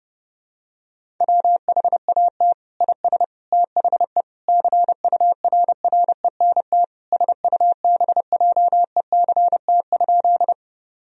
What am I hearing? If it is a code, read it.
Morse code, heard as a keyed tone, about 30 wpm, 700 Hz. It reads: WHAT IS THE CURRENT SUBJECT?
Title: Sentences from Top 1000 Words - Review Part 4 - 30wpm